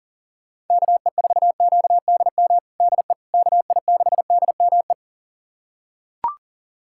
Morse code, heard as a keyed tone, about 37 wpm, 700 Hz. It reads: KE4QDM DE KI6DME
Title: USA Callsign DE Callsign - 6 Characters - 40wpm